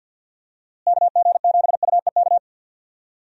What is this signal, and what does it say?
Morse code, heard as a keyed tone, about 51 wpm, 700 Hz. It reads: KG7UEK
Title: USA Call Signs - 6 Characters - Repeated 3x - 50wpm